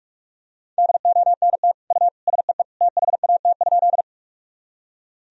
Morse code, heard as a keyed tone, about 46 wpm, 700 Hz. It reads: DONT U SEE THAT?
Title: Sentences from Top 300 Words and Abbreviations 45wpm